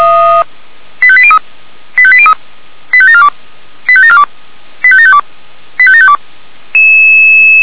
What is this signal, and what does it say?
Five-tone selective call (ZVEI), heard as quick runs of five tones, tones of about 70 ms. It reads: 85803 85803 85742 85742 85752 85752
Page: SDR/ZVEI